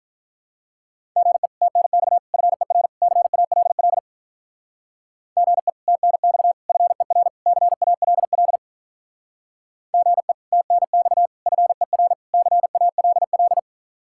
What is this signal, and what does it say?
Morse code, heard as a keyed tone, about 44 wpm, 700 Hz. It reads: GE TNX FER CALL GE TNX FER CALL GE TNX FER CALL
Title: Speed-Racing Common QSO Phrases - 35wpm